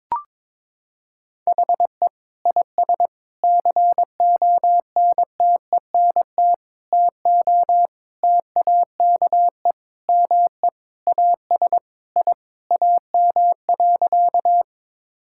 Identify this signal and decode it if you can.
Morse code, heard as a keyed tone, about 22 wpm, 700 Hz. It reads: HE IS CONTENT TO TAKE ME AS I AM.
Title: Sentences from Top 1000 Words - Review Part 6 - 22wpm